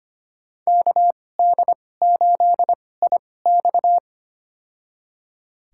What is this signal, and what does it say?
Morse code, heard as a keyed tone, about 25 wpm, spaced as for 18 wpm, 700 Hz. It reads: KD8IX
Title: USA Call Signs - 5 Characters - 2x Character Spacing - 25wpm